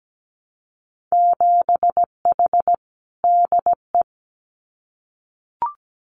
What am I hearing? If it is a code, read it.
Morse code, heard as a keyed tone, about 17 wpm, 700 Hz. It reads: 7H DE
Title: Winter Field Day Exchanges Encore 17wpm